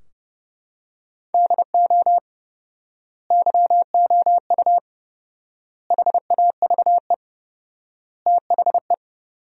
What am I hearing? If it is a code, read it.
Morse code, heard as a keyed tone, about 30 wpm, 700 Hz. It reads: DO YOU HAVE THE
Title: 4x Word-Spacing - Sentences from Top 300 Words - 30wpm